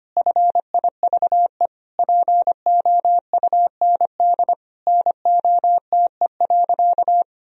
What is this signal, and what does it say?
Morse code, heard as a keyed tone, about 25 wpm, 700 Hz. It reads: FIVE POUND NOTE.
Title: Sentences from Top 1000 Words - Review Part 2 - 25wpm